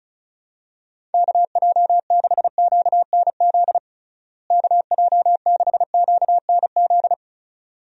Morse code, heard as a keyed tone, about 36 wpm, 700 Hz. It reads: KJ6QNZ KJ6QNZ
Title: USA Call Signs - 6 Characters - Repeated 2x - 35wpm